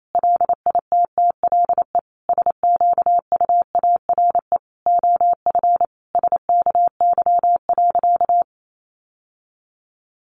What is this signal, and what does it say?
Morse code, heard as a keyed tone, about 28 wpm, 700 Hz. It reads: LITTLE SQUARE OF SKY.
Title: Sentences from Top 1200 Words 28wpm